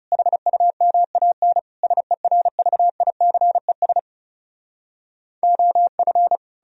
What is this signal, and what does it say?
Morse code, heard as a keyed tone, about 40 wpm, 700 Hz. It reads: HUMAN SERVICES OF